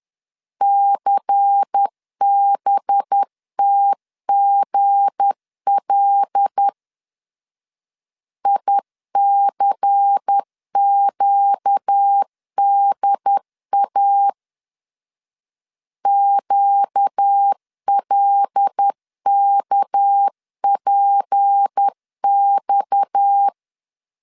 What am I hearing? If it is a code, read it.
Morse code, heard as a keyed tone, about 10 wpm, 800 Hz. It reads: CBTGL ICQDA QLKPX
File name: LEZIONI / 7 GF9Z / 7 GF9 10wpm 2747